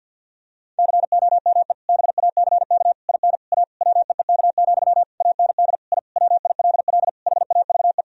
Morse code, heard as a keyed tone, about 52 wpm, 700 Hz. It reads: COME BACK IN A WEEK, AND I WILL HAVE
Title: Sentences from Top 1000 Words - Review Part 3 - 50wpm